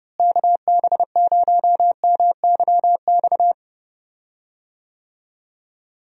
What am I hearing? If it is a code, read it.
Morse code, heard as a keyed tone, about 30 wpm, 700 Hz. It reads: KB0MYX
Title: USA Call Signs - 6 Characters - Repeated 3x - 30wpm